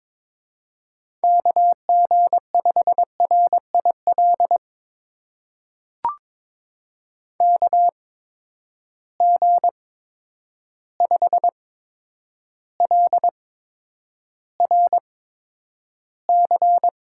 Morse code, heard as a keyed tone, about 22 wpm, spaced as for 9 wpm, 700 Hz. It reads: KG5RIL KG5LRC
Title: USA Call Signs - 6 Characters - 8x Character Spacing - 22wpm